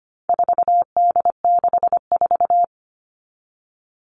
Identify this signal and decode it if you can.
Morse code, heard as a keyed tone, about 25 wpm, 700 Hz. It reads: 4D64
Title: All Cut Numbers - 4-Digit 25wpm